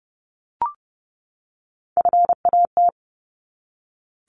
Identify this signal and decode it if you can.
Morse code, heard as a keyed tone, about 30 wpm, 700 Hz. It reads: FAT
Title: Top 501-750 Words 30wpm